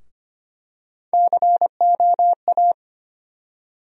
Morse code, heard as a keyed tone, about 25 wpm, 700 Hz. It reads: COA
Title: USA Call Sign Postfixes - 3 Letters 25wpm